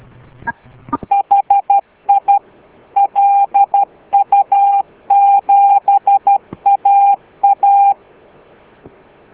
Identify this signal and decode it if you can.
Morse code, heard as a keyed tone, about 12 wpm, 800 Hz. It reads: HI LU7AA